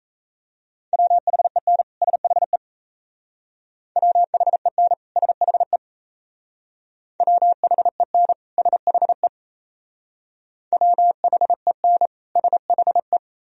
Morse code, heard as a keyed tone, about 35 wpm, 700 Hz. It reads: WHEN SHE WHEN SHE WHEN SHE WHEN SHE
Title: Speed-Racing - Sets of 2 Words 28wpm